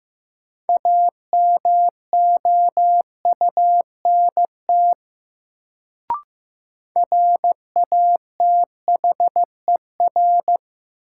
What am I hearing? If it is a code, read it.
Morse code, heard as a keyed tone, about 15 wpm, 700 Hz. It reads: AMOUNT RATHER
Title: Sentences from Top 700 Words - New Word Drill - 15wpm